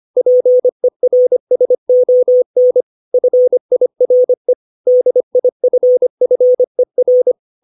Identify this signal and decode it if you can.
Morse code, heard as a keyed tone, about 25 wpm, 500 Hz. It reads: PERSON FIRE DIFFER
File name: 632 words / 632 Words 25WPM